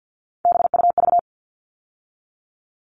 Morse code, heard as a keyed tone, about 51 wpm, 700 Hz. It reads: BUV
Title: All Cut Numbers - 3-Digit 50wpm